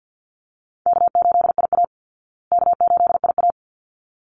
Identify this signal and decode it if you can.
Morse code, heard as a keyed tone, about 50 wpm, 700 Hz. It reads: K8IA K8IA
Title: Easy Call Signs Repeated 2x - 50wpm